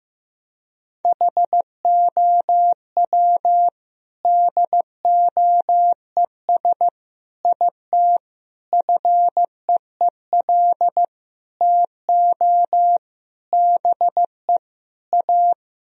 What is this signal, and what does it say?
Morse code, heard as a keyed tone, about 16 wpm, 700 Hz. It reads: HOW DOES IT FEEL TO BE A
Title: Sentences from Top 1000 Words - Review Part 2 - 15wpm